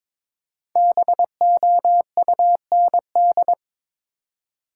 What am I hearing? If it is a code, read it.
Morse code, heard as a keyed tone, about 22 wpm, 700 Hz. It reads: BOUND
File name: 5 Letter Words - First 3 Letters Given - 22wpm